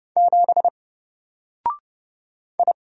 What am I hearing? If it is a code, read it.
Morse code, heard as a keyed tone, about 30 wpm, 700 Hz. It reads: 7 I
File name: Single-Letter-Number-30wpm